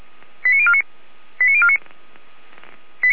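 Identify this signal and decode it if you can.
Five-tone selective call (ZVEI), heard as quick runs of five tones, tones of about 70 ms. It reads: 89049 89049
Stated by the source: ZVEI five-tone selective call 89305 received on an SDR by Feuerwehr Orlamünde.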